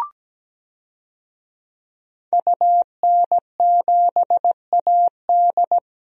Morse code, heard as a keyed tone, about 17 wpm, 700 Hz. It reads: UN7AD
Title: DXCC-100 Call Signs - 5 Characters - Repeated 3x - 17wpm